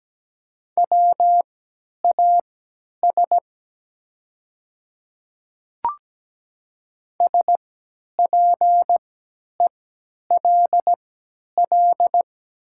Morse code, heard as a keyed tone, about 17 wpm, spaced as for 10 wpm, 700 Hz. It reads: WAS SPELL
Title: 3x Character Spacing - Top 200 Words - 17wpm